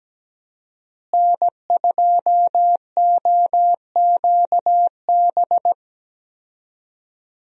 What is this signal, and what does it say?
Morse code, heard as a keyed tone, about 17 wpm, 700 Hz. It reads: N2OQB
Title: USA Call Signs - 5 Characters - Repeated 3x - 17wpm